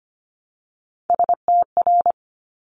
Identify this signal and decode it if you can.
Morse code, heard as a keyed tone, about 25 wpm, 700 Hz. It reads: STR
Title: Top Three Letter Combinations in Words 25wpm